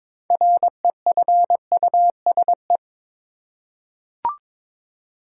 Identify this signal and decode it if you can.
Morse code, heard as a keyed tone, about 22 wpm, 700 Hz. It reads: REFUSE
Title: Six Letter Words - 22wpm